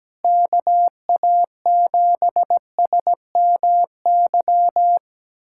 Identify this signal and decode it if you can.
Morse code, heard as a keyed tone, about 17 wpm, 700 Hz. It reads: KA7SMY
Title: USA Callsign DE Callsign - 6 Characters - 17wpm